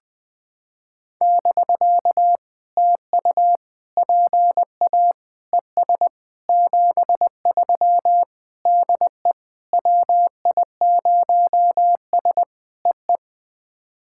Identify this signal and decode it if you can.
Morse code, heard as a keyed tone, about 20 wpm, 660 Hz. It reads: <BK> TU PA ES 73 DE WI0S EE
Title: POTA QSO Practice - 20wpm